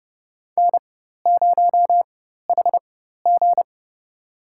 Morse code, heard as a keyed tone, about 30 wpm, spaced as for 14 wpm, 700 Hz. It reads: N0HG
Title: USA Call Signs - 4x Character Spacing - 30wpm